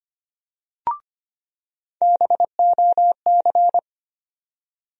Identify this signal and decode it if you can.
Morse code, heard as a keyed tone, about 25 wpm, 700 Hz. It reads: BOC
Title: USA Call Sign Postfixes - 3 Letters 25wpm